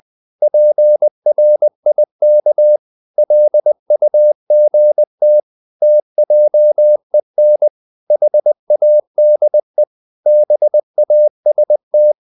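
Morse code, heard as a keyed tone, about 20 wpm, 600 Hz. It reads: PRIK LUGT TJEN HADE BAST
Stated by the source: Morse code practice of short Danish words at 20 wpm, from CW med Gnister.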